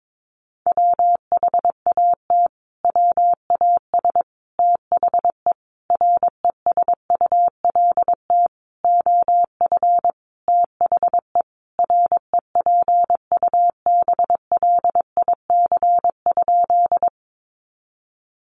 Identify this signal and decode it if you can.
Morse code, heard as a keyed tone, about 22 wpm, 700 Hz. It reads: WHAT WAS THE RESULT OF THE REPUBLIC?
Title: Sentences from Top 1500 Words 22wpm